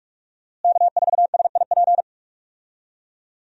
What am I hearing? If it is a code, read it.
Morse code, heard as a keyed tone, about 46 wpm, 700 Hz. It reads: K4SIP
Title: USA Call Signs - 5 Characters - Repeated 2x - 45wpm